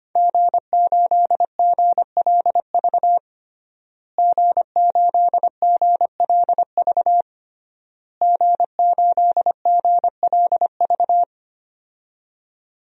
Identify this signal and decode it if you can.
Morse code, heard as a keyed tone, about 25 wpm, 700 Hz. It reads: G8GLV G8GLV G8GLV
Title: DXCC-100 Call Signs - 5 Characters - Repeated 3x - 25wpm